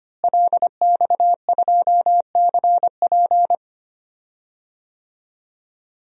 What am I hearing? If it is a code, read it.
Morse code, heard as a keyed tone, about 25 wpm, 700 Hz. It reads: LX2CP
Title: DXCC-100 Call Signs - 5 Characters - Repeated 2x - 25wpm